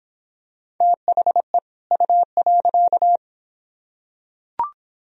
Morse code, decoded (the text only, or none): THE U.